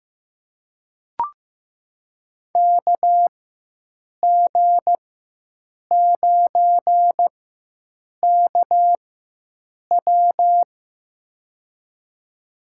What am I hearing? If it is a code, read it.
Morse code, heard as a keyed tone, about 15 wpm, spaced as for 7 wpm, 700 Hz. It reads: KG9KW
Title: USA Call Signs - 5 Characters - 4x Character Spacing - 15wpm